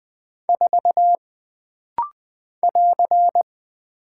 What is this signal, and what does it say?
Morse code, heard as a keyed tone, about 20 wpm, 700 Hz. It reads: 4 <AR>
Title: Single Character (Letter:Number:Punctuation:Prosign) - Rapid-Fire - 20wpm